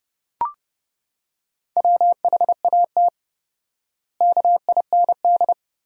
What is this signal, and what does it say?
Morse code, heard as a keyed tone, about 32 wpm, 700 Hz. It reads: WHAT KIND
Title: 4x Word-Spacing - Sentences from Top 500 Words - 30wpm